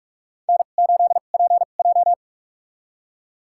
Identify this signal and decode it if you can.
Morse code, heard as a keyed tone, about 46 wpm, spaced as for 32 wpm, 700 Hz. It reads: N8PJ
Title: USA Call Signs - 4 Characters - 2x Character Spacing - 45wpm